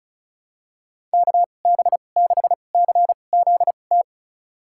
Morse code, heard as a keyed tone, about 34 wpm, spaced as for 25 wpm, 700 Hz. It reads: KB6CZT
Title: USA Call Signs - 2x Character Spacing - 35wpm